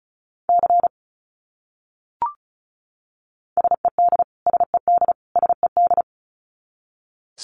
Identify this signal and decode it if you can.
Morse code, heard as a keyed tone, about 34 wpm, 700 Hz. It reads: C SED SED SED
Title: Abbreviations Repeated 3x - 35wpm